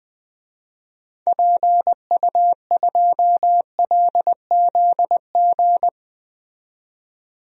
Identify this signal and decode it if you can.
Morse code, heard as a keyed tone, about 20 wpm, 700 Hz. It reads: PU2LZG
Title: DXCC-100 Call Signs - 6 Characters - 20wpm